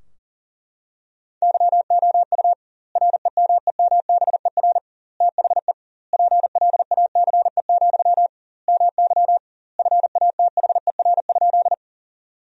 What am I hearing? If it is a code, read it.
Morse code, heard as a keyed tone, about 40 wpm, 700 Hz. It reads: YOU REMEMBER THE PLACE, MY FATHER?
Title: 2x Word-Spacing - Sentences from Top 400 Words - 40wpm